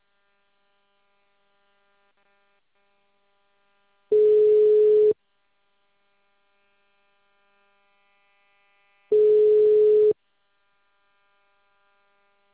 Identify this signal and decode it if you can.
Ringback tone (Europe, 425 Hz).
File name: out- -2044- - - .30924